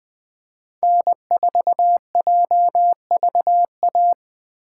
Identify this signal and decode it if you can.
Morse code, heard as a keyed tone, about 20 wpm, 700 Hz. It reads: N4JVA
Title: USA Call Signs - 5 Characters - 2x Character Spacing - 20wpm